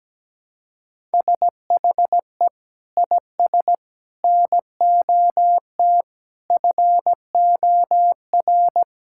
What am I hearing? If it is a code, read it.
Morse code, heard as a keyed tone, about 17 wpm, 700 Hz. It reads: SHE IS NOT FOR
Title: Sentences from Top 1000 Words - Review Part 5 - 17wpm